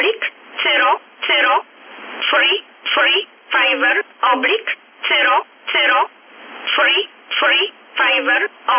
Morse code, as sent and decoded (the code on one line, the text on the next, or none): none